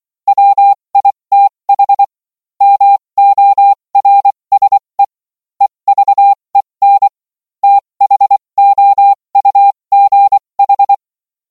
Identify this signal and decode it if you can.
Morse code, heard as a keyed tone, about 24 wpm, spaced as for 20 wpm, 800 Hz. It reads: WITH MORSE EVEN THOUGH